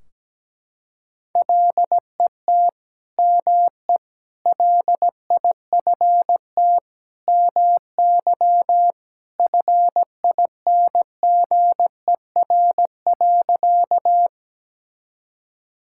Morse code, heard as a keyed tone, about 17 wpm, 700 Hz. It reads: LET ME LIFT MY FINGER.
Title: Sentences from Top 1600 Words 17wpm